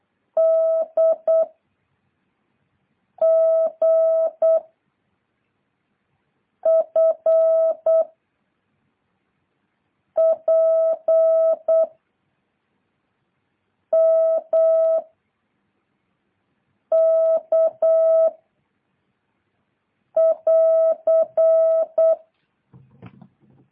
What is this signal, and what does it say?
Morse code, heard as a keyed tone, about 8 wpm, 660 Hz. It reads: D G F P M K <AR>